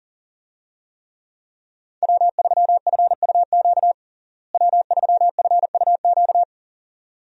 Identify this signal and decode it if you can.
Morse code, heard as a keyed tone, about 40 wpm, 700 Hz. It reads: W3FUQ W3FUQ
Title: USA Call Signs - 5 Characters - Repeated 2x - 40wpm